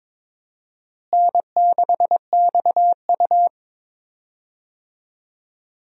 Morse code, heard as a keyed tone, about 21 wpm, 700 Hz. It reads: N6XU
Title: USA Call Signs - 4 Characters - Repeated 3x - 22wpm